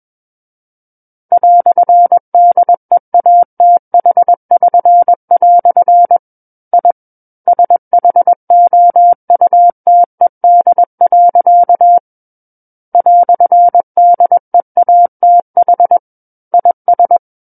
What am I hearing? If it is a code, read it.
Morse code, heard as a keyed tone, about 21 wpm, 700 Hz. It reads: "DEATH<SN>" I SHOUTED. "DEATH IS
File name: Book One / 12-Chapter 12 - 21 WPM